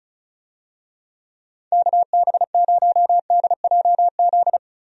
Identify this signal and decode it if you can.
Morse code, heard as a keyed tone, about 36 wpm, 700 Hz. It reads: KB0DJZ
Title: USA Call Signs - 6 Characters - 35wpm